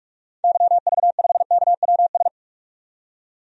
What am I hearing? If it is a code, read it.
Morse code, heard as a keyed tone, about 41 wpm, 700 Hz. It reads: YV5KWS
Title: DXCC-100 Call Signs - 6 Characters - 45wpm